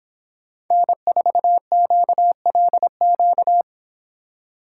Morse code, heard as a keyed tone, about 24 wpm, 700 Hz. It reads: N4QLQ